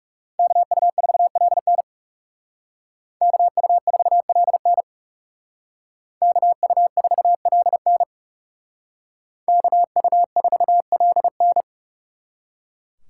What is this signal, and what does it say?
Morse code, heard as a keyed tone, about 36 wpm, 700 Hz. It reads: KU4LN KU4LN KU4LN KU4LN
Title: Speed Racing - USA 2x2 Call Signs 30wpm